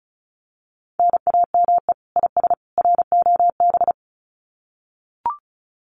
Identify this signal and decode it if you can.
Morse code, heard as a keyed tone, about 34 wpm, 700 Hz. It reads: NAME IS ROB
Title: Common QSO Phrases 35wpm